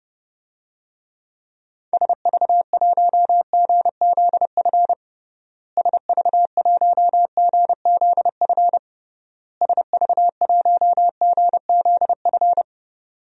Morse code, heard as a keyed tone, about 32 wpm, 700 Hz. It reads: SV1GZF SV1GZF SV1GZF
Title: DXCC-100 Call Signs - 6 Characters - Repeated 3x - 30wpm